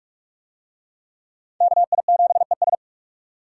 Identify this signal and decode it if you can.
Morse code, heard as a keyed tone, about 44 wpm, 700 Hz. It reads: KI7ES